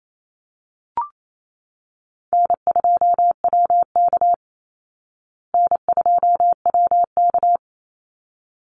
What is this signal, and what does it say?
Morse code, heard as a keyed tone, about 28 wpm, 700 Hz. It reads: N2WK N2WK
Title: Call Signs Repeated 3x - 28wpm